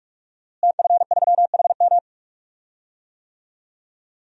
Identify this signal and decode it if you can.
Morse code, heard as a keyed tone, about 46 wpm, 700 Hz. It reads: TF3HM
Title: DXCC-100 Call Signs - 5 Characters - Repeated 2x - 45wpm